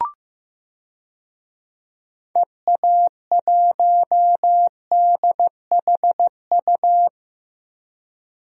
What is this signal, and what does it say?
Morse code, heard as a keyed tone, about 16 wpm, 700 Hz. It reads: EA1DHU
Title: DXCC-100 Call Signs - 6 Characters - Repeated 3x - 15wpm